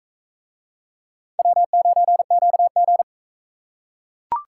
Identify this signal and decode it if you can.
Morse code, heard as a keyed tone, about 40 wpm, 700 Hz. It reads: W9QG